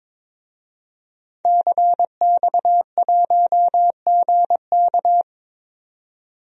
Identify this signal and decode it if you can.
Morse code, heard as a keyed tone, about 22 wpm, 700 Hz. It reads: CX1GK
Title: DXCC-100 Call Signs - 5 Characters - 22wpm